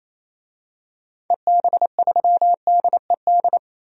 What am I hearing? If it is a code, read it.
Morse code, heard as a keyed tone, about 28 wpm, 700 Hz. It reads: EB3DED
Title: DXCC-100 Call Signs - 6 Characters - 28wpm